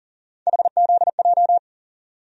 Morse code, heard as a keyed tone, about 38 wpm, 700 Hz. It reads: HZJ
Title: USA Call Sign Postfixes - 3 Letters 40wpm